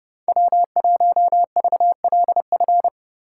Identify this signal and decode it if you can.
Morse code, heard as a keyed tone, about 30 wpm, 700 Hz. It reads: W1VLF